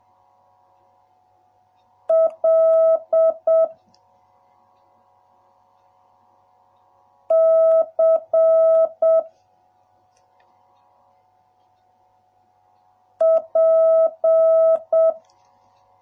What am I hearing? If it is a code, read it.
Morse code, heard as a keyed tone, about 7 wpm, 650 Hz. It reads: L C P